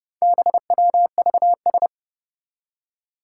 Morse code, heard as a keyed tone, about 30 wpm, 700 Hz. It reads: BWVS
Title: Four Challenging Letter-Number Combinations 30wpm